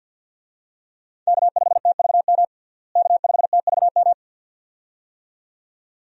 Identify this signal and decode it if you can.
Morse code, heard as a keyed tone, about 53 wpm, 700 Hz. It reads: K5TVM K5TVM
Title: USA Call Signs - 5 Characters - Repeated 2x - 50wpm